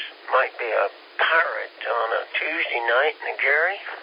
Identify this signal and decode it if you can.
Emergency Alert System (SAME) tones, heard as harsh buzzing bursts, with speech. It is heard from 1.1 to 3.8 s.